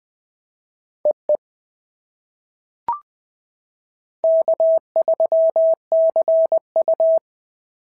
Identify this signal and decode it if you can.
Morse code, heard as a keyed tone, about 20 wpm, 690 Hz. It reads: EE K3CU
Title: POTA QSO Practice - 20wpm